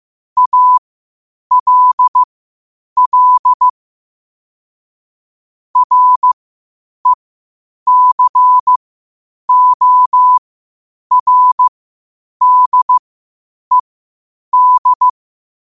Morse code, heard as a keyed tone, about 15 wpm, spaced as for 8 wpm, 1000 Hz. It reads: ALL RECORDED
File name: qotdfiles / 8wpm